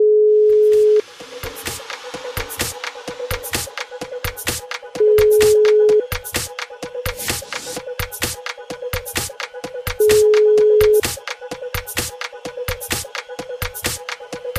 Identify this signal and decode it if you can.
Ringback tone (Europe, 425 Hz).